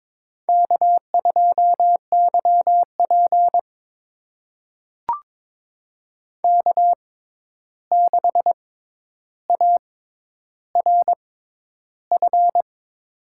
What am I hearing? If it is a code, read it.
Morse code, heard as a keyed tone, about 23 wpm, spaced as for 11 wpm, 700 Hz. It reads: K2YP K6ARF
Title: USA Call Signs - 6x Character Spacing - 22wpm